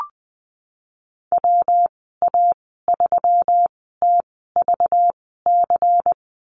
Morse code, heard as a keyed tone, about 20 wpm, spaced as for 15 wpm, 700 Hz. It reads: WA3TVC
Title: 213 Lesson 34g - Number 8 - Call Signs 20wpm